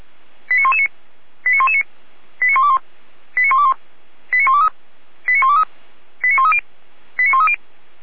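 Five-tone selective call (ZVEI), heard as quick runs of five tones, tones of about 70 ms. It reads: 89119 89119 89121 89121 89123 89123 89130 89130
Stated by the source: ZVEI five-tone selective call received by SDR on channel 455, recorded by Feuerwehr Orlamünde.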